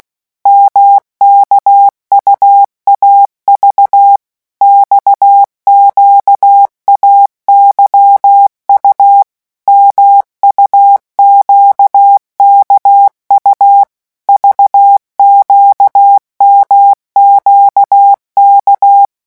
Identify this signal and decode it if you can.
Morse code, heard as a keyed tone, about 16 wpm, 800 Hz. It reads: MKUAV XQAYU MUQKU VQMQK
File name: morse mp3 / revision2-vitesse 15 mots